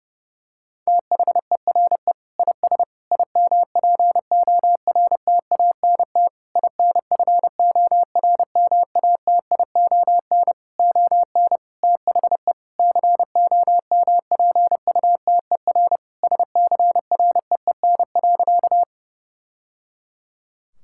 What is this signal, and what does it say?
Morse code, heard as a keyed tone, about 30 wpm, 700 Hz. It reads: THERE IS IMPORTANT INFORMATION ON THE COMPUTER SCREEN.
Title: Sentences from Top 1000 Words - Review Part 6 - 30wpm